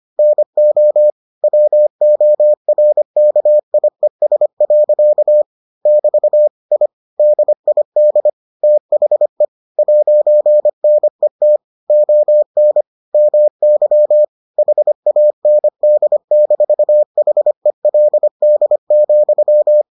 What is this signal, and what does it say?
Morse code, heard as a keyed tone, about 25 wpm, 600 Hz. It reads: NO WORKIES. <BT> I DID THE 'NET ON MY HAND-HELD,